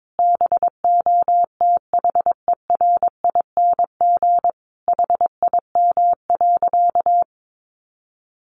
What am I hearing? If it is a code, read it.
Morse code, heard as a keyed tone, about 22 wpm, 700 Hz. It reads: BOTHERING HIM.
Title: Additional Easy Phrases 22wpm